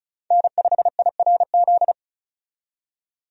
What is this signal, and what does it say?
Morse code, heard as a keyed tone, about 36 wpm, 700 Hz. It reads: N5IRZ
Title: USA Call Signs - 5 Characters - 35wpm